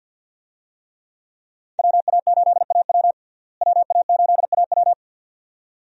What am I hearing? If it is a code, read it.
Morse code, heard as a keyed tone, about 48 wpm, 700 Hz. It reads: WA8AW WA8AW
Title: USA Call Signs - 5 Characters - Repeated 2x - 50wpm